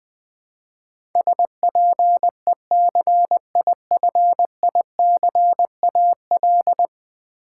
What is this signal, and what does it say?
Morse code, heard as a keyed tone, about 19 wpm, 700 Hz. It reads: SPECIFICAL
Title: Build A Word - 5 Letters At A Time - 12-Letter Words - 20wpm